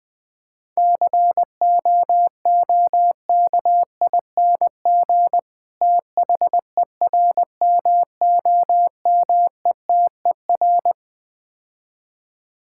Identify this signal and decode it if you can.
Morse code, heard as a keyed tone, about 20 wpm, 700 Hz. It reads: COOKING THERMOMETER
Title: Difficult Words - Speed-Racing with Hints - 20wpm